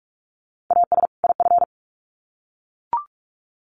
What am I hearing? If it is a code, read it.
Morse code, heard as a keyed tone, about 44 wpm, 700 Hz. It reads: AS IF
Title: Sets of 2 Words 45wpm